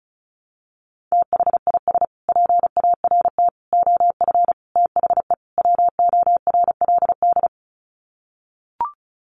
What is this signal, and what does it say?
Morse code, heard as a keyed tone, about 34 wpm, 700 Hz. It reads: THIS PART OF THE WORLD
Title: Sets of 5 Words Encore2 35wpm